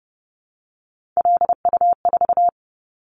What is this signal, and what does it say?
Morse code, heard as a keyed tone, about 30 wpm, 700 Hz. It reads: LU4
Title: Callsign Prefixes for Top 50 DXCC - 30wpm